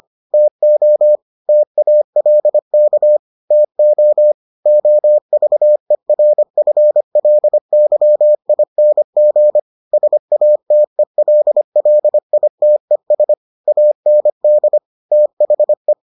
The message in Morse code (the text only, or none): TO TALK TO OVERFLYING SATELLITES AND THE